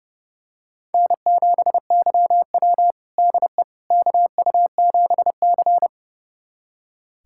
Morse code, heard as a keyed tone, about 30 wpm, 700 Hz. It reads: N7YW DE KU7C
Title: USA Callsign DE Callsign - 4 Characters - 30wpm